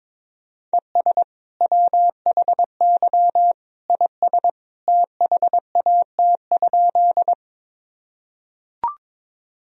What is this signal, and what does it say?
Morse code, heard as a keyed tone, about 22 wpm, 700 Hz. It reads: ES WHY IS THAT?
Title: Sentences from Top 200 Words and Abbreviations 22wpm